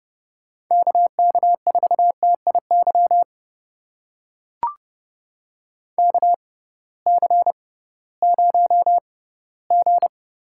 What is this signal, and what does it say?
Morse code, heard as a keyed tone, about 32 wpm, spaced as for 18 wpm, 700 Hz. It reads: KK4TIY KC0G
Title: USA Call Signs - 6 Characters - 6x Character Spacing - 30wpm